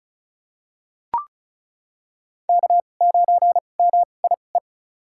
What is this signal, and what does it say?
Morse code, heard as a keyed tone, about 36 wpm, spaced as for 25 wpm, 700 Hz. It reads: K9MIE
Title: USA Call Signs - 2x Character Spacing - 35wpm